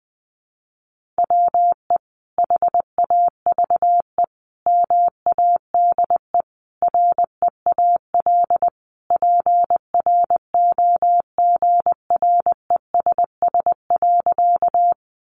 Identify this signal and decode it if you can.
Morse code, heard as a keyed tone, about 20 wpm, 700 Hz. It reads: WE HAVE MADE REAL PROGRESS.
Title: Sentences from Top 1400 Words 20wpm